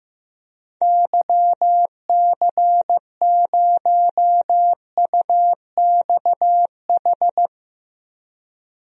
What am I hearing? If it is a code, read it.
Morse code, heard as a keyed tone, about 15 wpm, 700 Hz. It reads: YC0UXH
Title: DXCC-100 Call Signs - 6 Characters - 15wpm